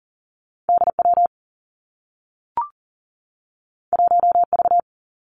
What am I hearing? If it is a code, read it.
Morse code, heard as a keyed tone, about 40 wpm, 700 Hz. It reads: DW 1V
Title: Two Challenging Letter-Number Combinations 40wpm